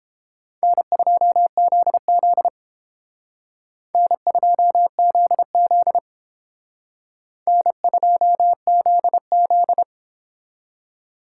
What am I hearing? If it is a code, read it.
Morse code, heard as a keyed tone, about 29 wpm, 700 Hz. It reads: N2ZZ N2ZZ N2ZZ